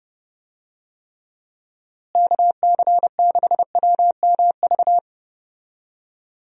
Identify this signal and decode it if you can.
Morse code, heard as a keyed tone, about 31 wpm, 700 Hz. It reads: KC6WMV
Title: USA Call Signs - 6 Characters - Repeated 3x - 30wpm